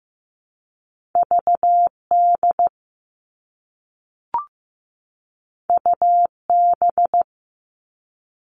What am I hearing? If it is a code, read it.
Morse code, heard as a keyed tone, about 15 wpm, 700 Hz. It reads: VD UB
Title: All Cut Numbers - 2-Digit 15wpm